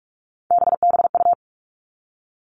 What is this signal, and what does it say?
Morse code, heard as a keyed tone, about 44 wpm, 700 Hz. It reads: BBU
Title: All Cut Numbers - 3-Digit 45wpm